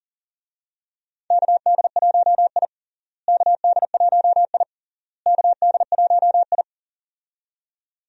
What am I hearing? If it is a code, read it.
Morse code, heard as a keyed tone, about 40 wpm, 700 Hz. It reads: KD1I KD1I KD1I
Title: USA Call Signs - 4 Characters - Repeated 3x - 40wpm